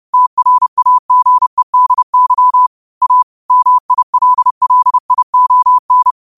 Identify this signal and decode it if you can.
Morse code, heard as a keyed tone, about 30 wpm, 1000 Hz. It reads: TRAGEDY A MILLION
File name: qotdfiles / 30wpm